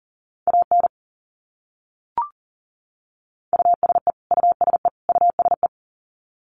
Morse code, heard as a keyed tone, about 40 wpm, 700 Hz. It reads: AN USE USE USE
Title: Top 200 Words Repeated 40wpm